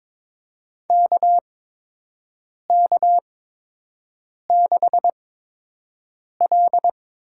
Morse code, heard as a keyed tone, about 22 wpm, spaced as for 6 wpm, 700 Hz. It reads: KK6L
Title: USA Call Signs - 6 Characters - 8x Character Spacing - 22wpm